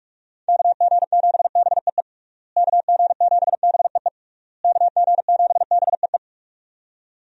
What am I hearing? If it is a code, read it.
Morse code, heard as a keyed tone, about 47 wpm, 700 Hz. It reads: KG7BEE KG7BEE KG7BEE
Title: USA Call Signs - 6 Characters - Repeated 3x - 45wpm